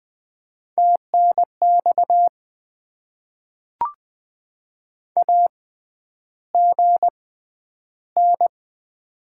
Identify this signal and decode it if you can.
Morse code, heard as a keyed tone, about 20 wpm, spaced as for 10 wpm, 700 Hz. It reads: TNX AGN
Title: Top 20 QSO Elements - 6x Character Spacing - 20wpm